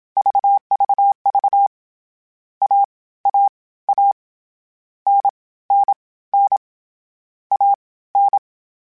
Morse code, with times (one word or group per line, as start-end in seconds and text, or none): 0.2-1.7 VVV
2.6-4.1 AAA
5.1-6.6 NNN
7.5-8.4 AN